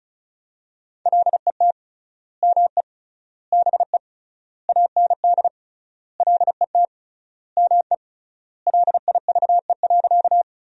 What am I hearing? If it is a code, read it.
Morse code, heard as a keyed tone, about 36 wpm, 700 Hz. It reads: LET ME BE AND LET ME LIVE.
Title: 3x Word-Spacing - Sentences from Top 300 Words - 35wpm